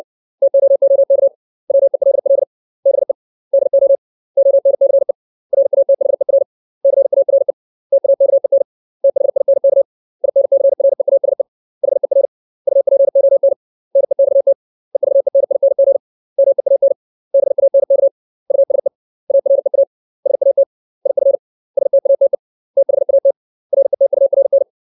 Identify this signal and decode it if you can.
Morse code, heard as a keyed tone, about 60 wpm, 550 Hz. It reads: TOOK WELL BE DO ONCE RATHER CARE TAKEN THINK INCREASE HER UPON NEXT EVENING MEAN BANK USE AREA SAT IF STATE THAT RETURN
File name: 632 words / 632 Words 60WPM